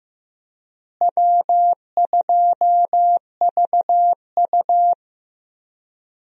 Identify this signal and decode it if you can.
Morse code, heard as a keyed tone, about 15 wpm, 700 Hz. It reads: W2VU
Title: USA Call Signs - 4 Characters - Repeated 2x - 15wpm